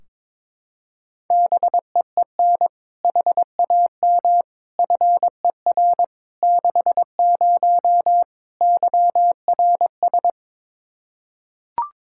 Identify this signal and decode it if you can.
Morse code, heard as a keyed tone, about 22 wpm, 700 Hz. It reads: BEEN HAM FER 60 YRS
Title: K1USN SST QSO Practice 22wpm